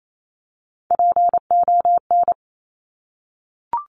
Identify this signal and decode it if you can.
Morse code, heard as a keyed tone, about 28 wpm, 700 Hz. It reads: PON
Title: Top Three Letter Combinations in Words 28wpm